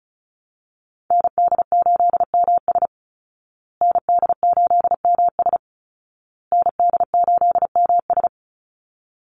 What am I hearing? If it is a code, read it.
Morse code, heard as a keyed tone, about 34 wpm, 700 Hz. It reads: ND8MS ND8MS ND8MS
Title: Call Signs Repeated 2x - 35wpm